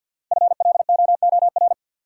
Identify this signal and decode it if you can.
Morse code, heard as a keyed tone, about 51 wpm, 700 Hz. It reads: FLOOR
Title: Sets of 3 Words - New Word Drill - 50wpm